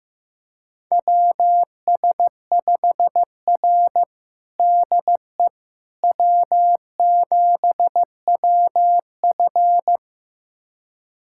Morse code, heard as a keyed tone, about 16 wpm, 700 Hz. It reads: WS5R DE W7WF
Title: USA Callsign DE Callsign - 4 Characters - 15wpm